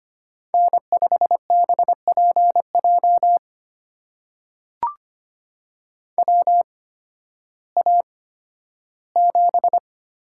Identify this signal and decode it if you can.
Morse code, heard as a keyed tone, about 25 wpm, spaced as for 13 wpm, 700 Hz. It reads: N5BPJ WA7
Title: USA Call Signs - 5 Characters - 8x Character Spacing - 25wpm